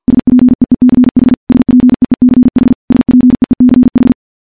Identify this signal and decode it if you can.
Morse code, heard as a keyed tone, about 50 wpm, 250 Hz. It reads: SPEECH SPEECH SPEECH
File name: morse / morse-training-50WPM